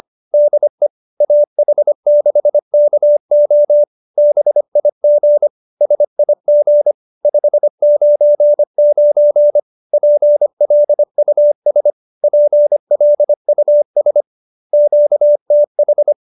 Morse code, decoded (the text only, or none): DE AH6KO BIG SIG 599 PLUS PLUS QTH